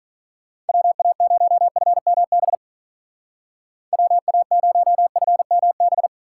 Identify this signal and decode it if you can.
Morse code, heard as a keyed tone, about 45 wpm, 700 Hz. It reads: WA0FMB WA0FMB